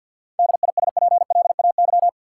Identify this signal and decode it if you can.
Morse code, heard as a keyed tone, about 53 wpm, 700 Hz. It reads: DISPLAY
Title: Seven Letter Words - 50wpm